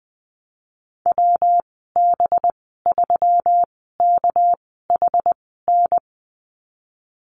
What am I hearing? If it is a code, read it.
Morse code, heard as a keyed tone, about 19 wpm, 700 Hz. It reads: W B 3 K H N
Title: 207 Lesson 33g - Letter j - Call Signs 20wpm